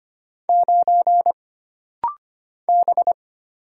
Morse code, heard as a keyed tone, about 25 wpm, 700 Hz. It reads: 9 B
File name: Single Character (Letter:Number:Punctuation:Prosign) - Rapid-Fire - 25wpm